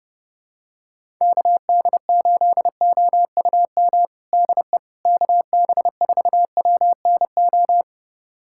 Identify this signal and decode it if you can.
Morse code, heard as a keyed tone, about 32 wpm, 700 Hz. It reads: KD8OUM DE KB4WNO
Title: USA Callsign DE Callsign 30wpm